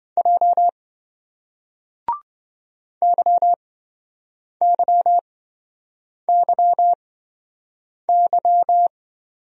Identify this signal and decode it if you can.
Morse code, heard as a keyed tone, about 26 wpm, spaced as for 7 wpm, 700 Hz. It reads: J YYYY